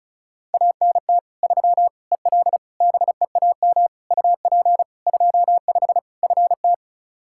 Morse code, heard as a keyed tone, about 34 wpm, 700 Hz. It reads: ANT 3 EL BEAM UP 25 FT
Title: Practice QSO - Exchange 2 - Info - 35wpm